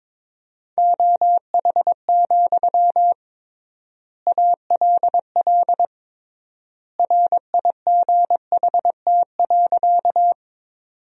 Morse code, heard as a keyed tone, about 22 wpm, 700 Hz. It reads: OH, ALL RIGHT.
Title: 3x Word-Spacing - Sentences from Top 500 Words - 22wpm